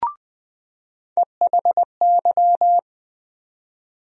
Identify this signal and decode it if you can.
Morse code, heard as a keyed tone, about 20 wpm, 700 Hz. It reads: EHY
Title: USA Call Sign Postfixes - 3 Letters 20wpm